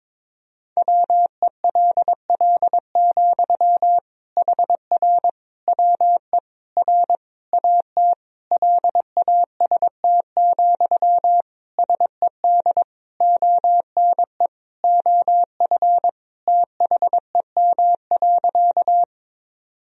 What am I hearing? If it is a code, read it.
Morse code, heard as a keyed tone, about 22 wpm, 700 Hz. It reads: WELL, HR WE R AT LAST, SED ONE OF THEM.
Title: Sentences from Top 300 Words and Abbreviations 22wpm